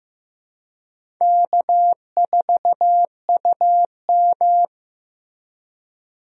Morse code, heard as a keyed tone, about 15 wpm, 700 Hz. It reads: K4UM
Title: USA Call Signs - 4 Characters - Repeated 3x - 15wpm